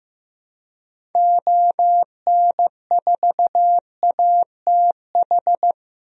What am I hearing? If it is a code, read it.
Morse code, heard as a keyed tone, about 16 wpm, 700 Hz. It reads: ON4ATH
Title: DXCC-100 Call Signs - 6 Characters - Repeated 2x - 15wpm